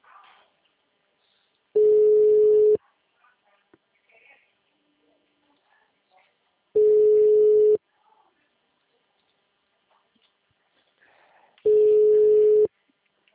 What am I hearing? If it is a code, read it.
Ringback tone (Europe, 425 Hz), with speech.